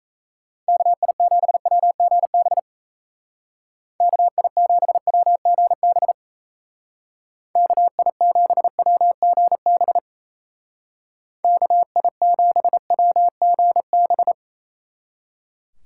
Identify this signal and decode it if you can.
Morse code, heard as a keyed tone, about 35 wpm, 700 Hz. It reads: KI7WGB KI7WGB KI7WGB KI7WGB
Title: Speed Racing - USA Call Signs 28wpm